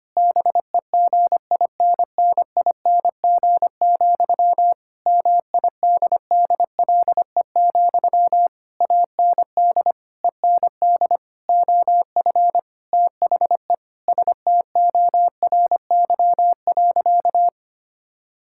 Morse code, heard as a keyed tone, about 25 wpm, 700 Hz. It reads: BEGINNING, MIDDLE, AND END OF THE STORY.
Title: Sentences from Top 1000 Words - Review Part 6 - 25wpm